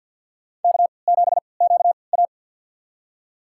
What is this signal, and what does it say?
Morse code, heard as a keyed tone, about 53 wpm, spaced as for 29 wpm, 700 Hz. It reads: K7QA
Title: USA Call Signs - 4 Characters - 3x Character Spacing - 50wpm